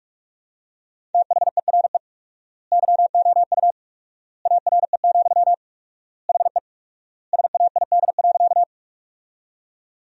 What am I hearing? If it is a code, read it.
Morse code, heard as a keyed tone, about 41 wpm, 700 Hz. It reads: THERE YOU ARE, HE SAID.
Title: 4x Word-Spacing - Sentences from Top 100 Words - 45wpm